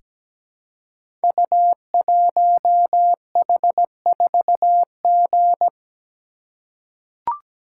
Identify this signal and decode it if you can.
Morse code, heard as a keyed tone, about 17 wpm, 690 Hz. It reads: U1H4G
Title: Code Groups - 5 Letter:Numbers - 17wpm